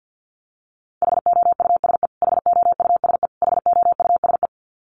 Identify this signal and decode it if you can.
Morse code, heard as a keyed tone, about 50 wpm, 700 Hz. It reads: HOUSE HOUSE HOUSE
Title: Top 500 Words Repeated 50wpm